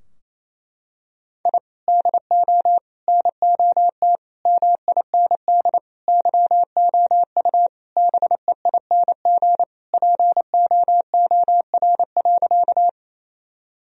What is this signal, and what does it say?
Morse code, heard as a keyed tone, about 28 wpm, 700 Hz. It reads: I DO NOT MIND YOU BEING POOR.
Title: Sentences from Top 1000 Words - Review Part 3 - 28wpm